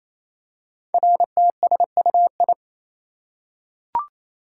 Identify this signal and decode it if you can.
Morse code, heard as a keyed tone, about 29 wpm, 700 Hz. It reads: RTSUI
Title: Code Groups - 5 Letters - 28wpm